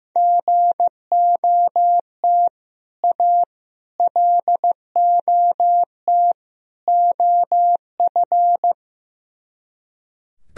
Morse code, heard as a keyed tone, about 16 wpm, 700 Hz. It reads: GOT A LOT OF
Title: Speed-Racing - Sets of 4 Words 15wpm